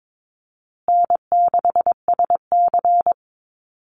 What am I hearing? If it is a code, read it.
Morse code, heard as a keyed tone, about 22 wpm, 700 Hz. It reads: N6SC
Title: USA Call Signs Repeated 3x 22wpm